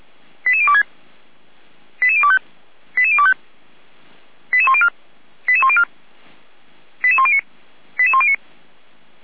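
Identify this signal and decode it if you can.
Five-tone selective call (ZVEI), heard as quick runs of five tones, tones of about 70 ms. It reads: 88026 88025 88025 88104 88104 88109 88109
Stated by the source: ZVEI five-tone call 88102, received by SDR; from feuerwehr-orlamuende.org fire brigade server.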